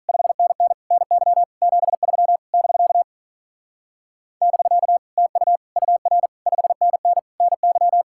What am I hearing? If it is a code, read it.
Morse code, heard as a keyed tone, about 43 wpm, 700 Hz. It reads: 5NN NY 73 <BK> <BK> TU UR 5NN NY
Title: Speed-Racing Common QSO Phrases - 35wpm